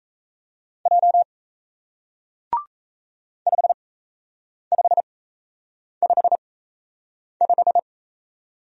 Morse code, heard as a keyed tone, about 38 wpm, spaced as for 8 wpm, 700 Hz. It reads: J 5555